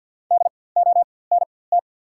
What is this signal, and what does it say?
Morse code, heard as a keyed tone, about 48 wpm, spaced as for 23 wpm, 690 Hz. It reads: DONT
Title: 4x Character Spacing - Top 300 Words - 50wpm